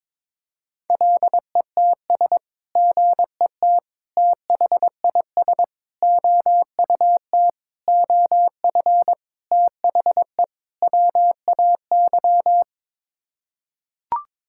Morse code, heard as a keyed tone, about 22 wpm, 700 Hz. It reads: LETS GET THIS OUT OF THE WAY
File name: practice / 22wpm / Sets of 7 Words - 22wpm